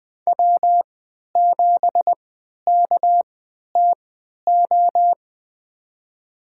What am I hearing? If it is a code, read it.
Morse code, heard as a keyed tone, about 20 wpm, spaced as for 11 wpm, 700 Hz. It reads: W7KTO
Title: USA Call Signs - 5 Characters - 3x Character Spacing - 20wpm